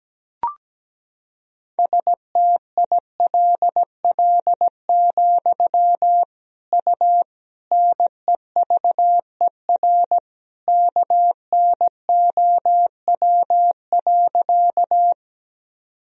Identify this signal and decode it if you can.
Morse code, heard as a keyed tone, about 17 wpm, 700 Hz. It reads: STILL, U NEVER KNOW.
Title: Sentences from Top 300 Words and Abbreviations 17wpm